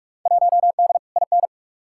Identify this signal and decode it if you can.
Morse code, heard as a keyed tone, about 43 wpm, 700 Hz. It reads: 1D IN
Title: Field Day Exchanges - 45wpm